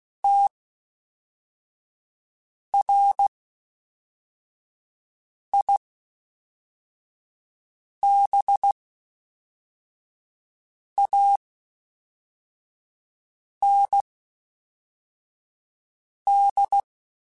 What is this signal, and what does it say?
Morse code, heard as a keyed tone, about 16 wpm, spaced as for 4 wpm, 800 Hz. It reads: TRIBAND
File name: (spaced) / FromSMs spaced mp3s / Classes@5